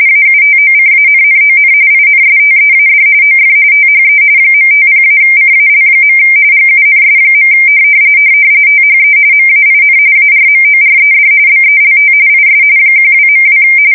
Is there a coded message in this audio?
Yes: RTTY from 0.1 to 13.6 s, about 45.45 baud. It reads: AUTOCANNON INSTEAD OF THE .50-CAL. GUNS. THE MAIN ARMORED BELT IS 12.2